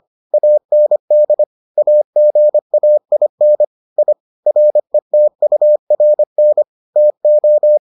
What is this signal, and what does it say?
Morse code, heard as a keyed tone, about 25 wpm, 600 Hz. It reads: AND AGAIN I RETURN TO